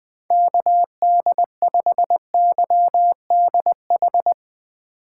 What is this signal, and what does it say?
Morse code, heard as a keyed tone, about 20 wpm, 700 Hz. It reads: KD5YDH
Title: USA Call Signs - 6 Characters - 3x Character Spacing - 20wpm